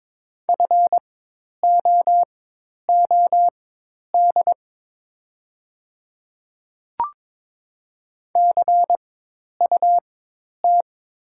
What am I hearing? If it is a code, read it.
Morse code, heard as a keyed tone, about 22 wpm, spaced as for 10 wpm, 700 Hz. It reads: FOOD CUT
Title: 4x Character Spacing - Top 300 Words - 22wpm